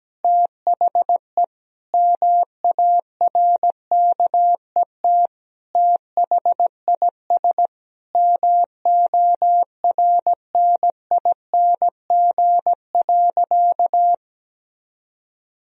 Morse code, heard as a keyed tone, about 17 wpm, 700 Hz. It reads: THE MARKET THIS MORNING.
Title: Sentences from Top 1000 Words - Review Part 3 - 17wpm